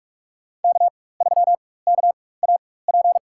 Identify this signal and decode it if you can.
Morse code, heard as a keyed tone, about 46 wpm, spaced as for 21 wpm, 700 Hz. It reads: K3KAP
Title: USA Call Signs - 5 Characters - 4x Character Spacing - 45wpm